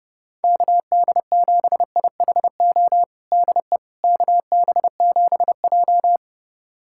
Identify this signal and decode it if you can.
Morse code, heard as a keyed tone, about 30 wpm, 700 Hz. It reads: KD7IHO DE KB7J
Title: USA Callsign DE Callsign 30wpm